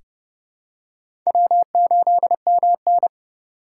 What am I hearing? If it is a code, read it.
Morse code, heard as a keyed tone, about 31 wpm, 700 Hz. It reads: W8MN
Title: USA Call Signs - 4 Characters - 6x Character Spacing - 30wpm